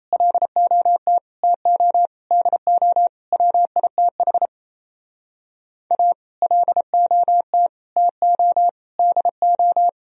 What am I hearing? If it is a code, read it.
Morse code, heard as a keyed tone, about 36 wpm, 700 Hz. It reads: LOT TO DO WITH A LOT TO DO